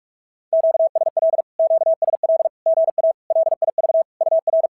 Morse code, heard as a keyed tone, about 46 wpm, 690 Hz. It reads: QSL QSL GA LIV UR
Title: SOTA QSO Practice 45wpm